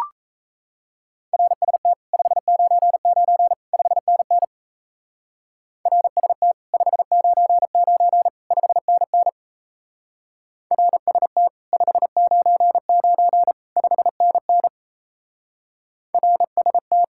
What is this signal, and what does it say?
Morse code, heard as a keyed tone, about 33 wpm, 700 Hz. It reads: RST 599 5NN RST 599 5NN RST 599 5NN RST
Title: Speed-Racing Common QSO Phrases Encore - 28wpm